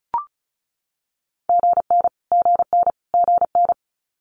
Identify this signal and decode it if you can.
Morse code, heard as a keyed tone, about 33 wpm, 700 Hz. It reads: GN GN GN
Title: Multi-letter Phonemes Repeated 3x 35wpm